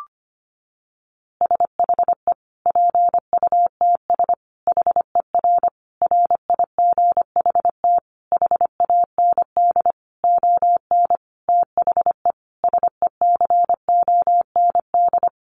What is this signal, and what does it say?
Morse code, heard as a keyed tone, about 25 wpm, 700 Hz. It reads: SHE PUTS HER RIGHT HAND ON THE SECOND
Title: Easy Phrases 25wpm